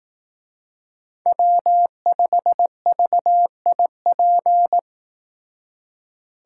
Morse code, heard as a keyed tone, about 18 wpm, 700 Hz. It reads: W5VIP
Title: Speed Racing - USA 1x3 Call Signs 15wpm